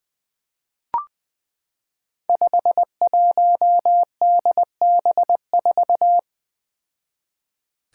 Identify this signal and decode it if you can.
Morse code, heard as a keyed tone, about 20 wpm, 700 Hz. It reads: 51DB4
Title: Five Challenging Letter-Number Combinations 20wpm